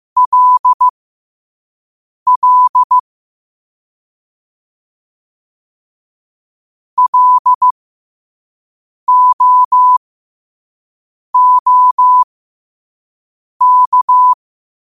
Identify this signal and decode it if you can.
Morse code, heard as a keyed tone, about 15 wpm, spaced as for 5 wpm, 1000 Hz. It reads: LL LOOK
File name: qotdfiles / 5wpm